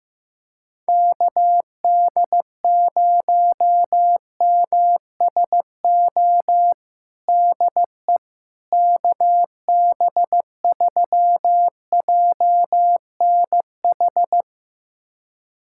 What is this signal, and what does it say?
Morse code, heard as a keyed tone, about 15 wpm, 700 Hz. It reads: KD0MSO DE KB3JNH
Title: USA Callsign DE Callsign - 6 Characters - 15wpm